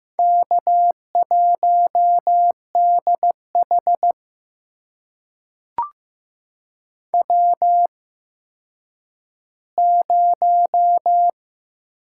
Morse code, heard as a keyed tone, about 15 wpm, spaced as for 9 wpm, 700 Hz. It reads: K1DH W0
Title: USA Call Signs - 4 Characters - 8x Character Spacing - 15wpm